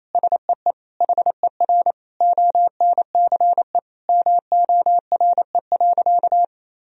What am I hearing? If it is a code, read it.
Morse code, heard as a keyed tone, about 28 wpm, 700 Hz. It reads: SEE HER ONCE MORE.
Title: Sentences from Top 500 Words and Abbreviations 28wpm